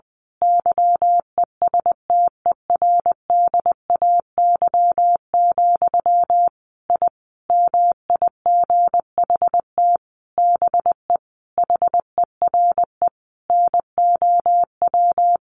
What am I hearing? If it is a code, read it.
Morse code, heard as a keyed tone, about 20 wpm, 700 Hz. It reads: YESTERDAY, I MIGHT BE HERE NOW
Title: A Sound Of Thunder 20wpm